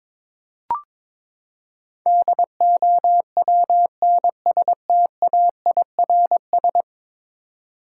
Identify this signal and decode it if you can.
Morse code, heard as a keyed tone, about 22 wpm, 700 Hz. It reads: DOWNSTAIRS
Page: morse-code-ninja-practice-22wpm